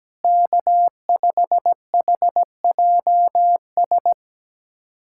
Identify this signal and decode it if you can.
Morse code, heard as a keyed tone, about 17 wpm, 700 Hz. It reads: K5HJS
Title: USA Callsign DE Callsign - 5 Characters - 17wpm